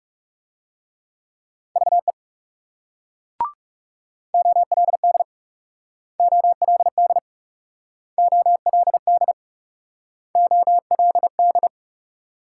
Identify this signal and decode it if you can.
Morse code, heard as a keyed tone, about 37 wpm, 700 Hz. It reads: VE OLD OLD OLD OLD
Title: Speed-Racing Multi-letter Phonemes - 30wpm